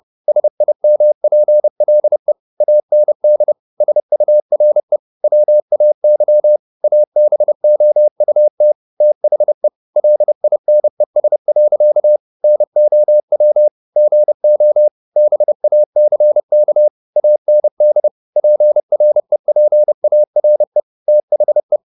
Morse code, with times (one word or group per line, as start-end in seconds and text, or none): 0.3-2.3 SIMPLE
2.6-3.5 AND
3.8-5.0 SURE
5.2-6.6 WAY
6.8-8.7 ABOUT
9.0-9.7 THE
10.0-12.2 LINES.
12.4-13.7 NOW
14.0-14.9 GO
15.2-16.9 BACK
17.2-18.1 AND
18.4-20.8 PREPARE
21.1-21.8 THE